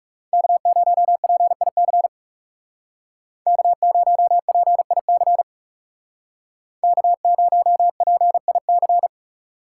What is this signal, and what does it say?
Morse code, heard as a keyed tone, about 37 wpm, 700 Hz. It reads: K0PIC K0PIC K0PIC
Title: Speed Racing - USA Call Signs 30wpm